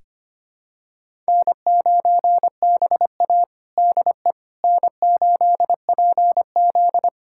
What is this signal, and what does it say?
Morse code, heard as a keyed tone, about 25 wpm, 700 Hz. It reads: N9BA DE N8PZ
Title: USA Callsign DE Callsign - 4 Characters - 25wpm